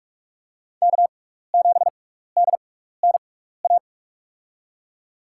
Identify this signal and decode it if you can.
Morse code, heard as a keyed tone, about 46 wpm, spaced as for 15 wpm, 700 Hz. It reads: K7DNA
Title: USA Call Signs - 5 Characters - 6x Character Spacing - 45wpm